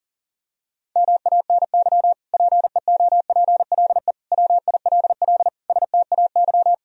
Morse code, heard as a keyed tone, about 40 wpm, 700 Hz. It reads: MANY PEOPLE WILL STAY
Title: Sentences from Top 1000 Words - Review Part 1 - 40wpm